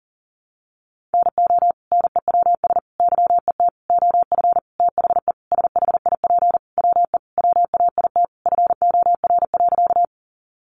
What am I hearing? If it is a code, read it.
Morse code, heard as a keyed tone, about 40 wpm, 700 Hz. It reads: NO NEWS YET OF THE SHIP WE WAIT FOR.
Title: Sentences from Top 800 Words 40wpm